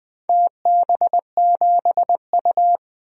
Morse code, heard as a keyed tone, about 20 wpm, 700 Hz. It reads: TB7U
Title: DXCC-100 Call Signs - 4 Characters - 20wpm